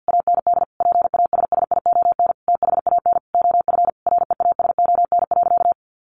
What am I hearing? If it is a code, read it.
Morse code, heard as a keyed tone, about 50 wpm, 700 Hz. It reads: AND PASSION THAN OF REASON.
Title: Sentences from Top 1300 Words 50wpm